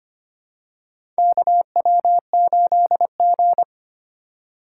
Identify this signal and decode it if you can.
Morse code, heard as a keyed tone, about 25 wpm, 700 Hz. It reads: KW8G
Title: USA Call Signs - 4x Character Spacing - 25wpm